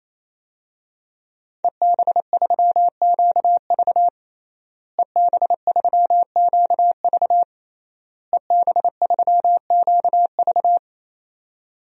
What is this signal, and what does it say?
Morse code, heard as a keyed tone, about 28 wpm, 700 Hz. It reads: EB3QV EB3QV EB3QV
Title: DXCC-100 Call Signs - 5 Characters - Repeated 3x - 28wpm